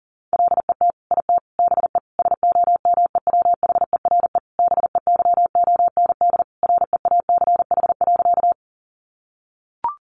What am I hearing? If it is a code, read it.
Morse code, heard as a keyed tone, about 40 wpm, 700 Hz. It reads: LET IT BE SOMEWHERE BEYOND REACH.
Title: Sentences from Top 1400 Words 40wpm